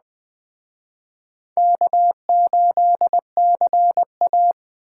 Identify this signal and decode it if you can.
Morse code, heard as a keyed tone, about 20 wpm, 700 Hz. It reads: K8CA